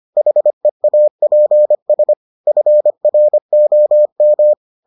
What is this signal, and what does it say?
Morse code, heard as a keyed tone, about 25 wpm, 600 Hz. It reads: HEAPS FROM